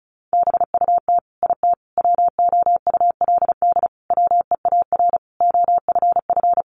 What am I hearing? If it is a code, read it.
Morse code, heard as a keyed tone, about 34 wpm, 700 Hz. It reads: BUT IT WOULD WEAR OFF
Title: Sentences from Top 1500 Words 35wpm